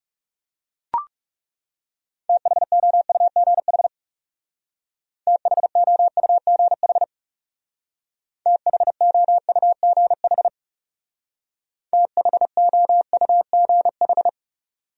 Morse code, heard as a keyed tone, about 36 wpm, 700 Hz. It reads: THOUGH THOUGH THOUGH THOUGH
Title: Speed-Racing Top 500 Words 30wpm